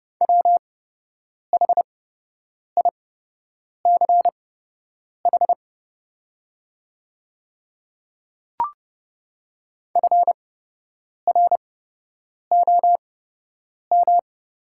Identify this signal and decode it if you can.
Morse code, heard as a keyed tone, about 29 wpm, spaced as for 8 wpm, 700 Hz. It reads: WHICH FROM
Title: 8x Character Spacing - Top 300 Words - 30wpm